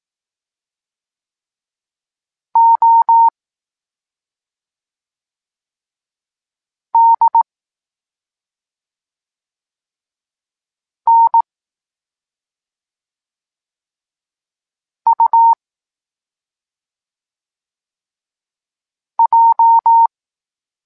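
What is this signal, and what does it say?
Morse code, heard as a keyed tone, about 18 wpm, 920 Hz. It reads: O D N U J
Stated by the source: Evenly spaced Morse lesson text 21 for learning letters G and Z, from SP9MOA Niepołomicki Klub Krótkofalowców.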